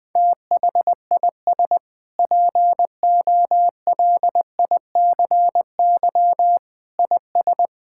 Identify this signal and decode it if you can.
Morse code, heard as a keyed tone, about 20 wpm, 700 Hz. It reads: THIS POLICY IS
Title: Sentences with 6-Word Phrases - Spoken Beginning - 20wpm